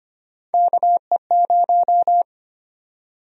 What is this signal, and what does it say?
Morse code, heard as a keyed tone, about 25 wpm, 700 Hz. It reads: KE0
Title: USA Call Sign Prefixes - 25wpm